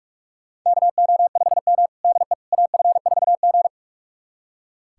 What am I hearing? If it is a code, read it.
Morse code, heard as a keyed tone, about 41 wpm, 700 Hz. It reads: KO5M DE AF4G
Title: USA Callsign DE Callsign - 4 Characters - 45wpm